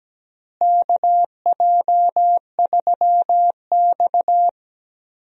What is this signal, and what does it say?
Morse code, heard as a keyed tone, about 17 wpm, 700 Hz. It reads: KJ3X
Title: USA Call Signs - 4 Characters - Repeated 2x - 17wpm